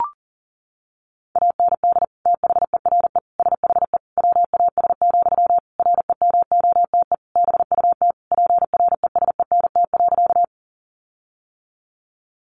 Morse code, heard as a keyed tone, about 40 wpm, 700 Hz. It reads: AND THERE SHE WAS, REMOTE BUT PRESENT.
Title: Sentences from Top 1300 Words 40wpm